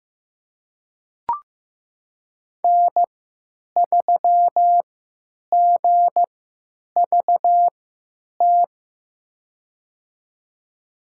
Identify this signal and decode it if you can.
Morse code, heard as a keyed tone, about 15 wpm, spaced as for 9 wpm, 700 Hz. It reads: N3GVT
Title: USA Call Signs - 5 Characters - 3x Character Spacing - 15wpm